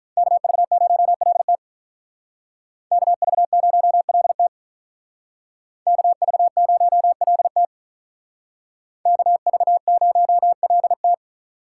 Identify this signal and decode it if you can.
Morse code, heard as a keyed tone, about 46 wpm, 700 Hz. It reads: KV0LT KV0LT KV0LT KV0LT